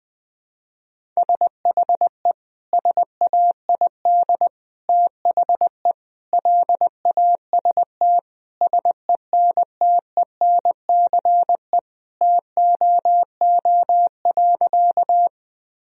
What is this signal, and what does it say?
Morse code, heard as a keyed tone, about 20 wpm, 700 Hz. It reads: SHE SAID THE LAST SENTENCE TOO.
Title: Sentences from Top 1000 Words - Review Part 1 - 20wpm